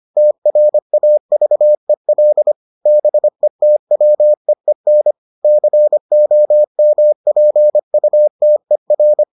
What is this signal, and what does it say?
Morse code, heard as a keyed tone, about 25 wpm, 600 Hz. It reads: TRAVEL BETWEEN COMPUTER